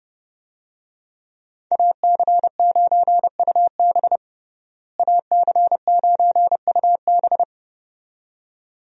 Morse code, heard as a keyed tone, about 28 wpm, 700 Hz. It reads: AC9UB AC9UB
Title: USA Call Signs - 5 Characters - Repeated 2x - 30wpm